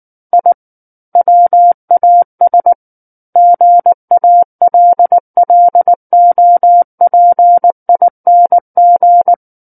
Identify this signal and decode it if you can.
Morse code, heard as a keyed tone, about 19 wpm, 700 Hz. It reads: I WAS GALLOPING
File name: Book One / 10-Chapter 10 - 19 WPM